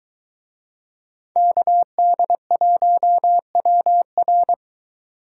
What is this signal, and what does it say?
Morse code, heard as a keyed tone, about 23 wpm, 700 Hz. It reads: KD1WR